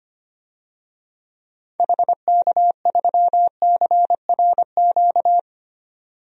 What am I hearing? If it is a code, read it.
Morse code, heard as a keyed tone, about 25 wpm, 700 Hz. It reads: HK3CRQ
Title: DXCC-100 Call Signs - 6 Characters - 25wpm